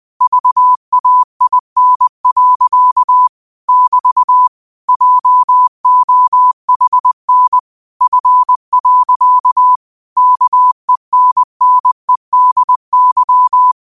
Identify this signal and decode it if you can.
Morse code, heard as a keyed tone, about 20 wpm, 1000 Hz. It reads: VAIN. <BT> JOHN F. KENNEDY
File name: qotdfiles / 20wpm